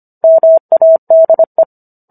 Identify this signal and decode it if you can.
Morse code, heard as a keyed tone, about 25 wpm, 650 Hz. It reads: MADE